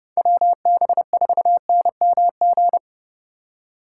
Morse code, heard as a keyed tone, about 34 wpm, 700 Hz. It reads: WB4NMG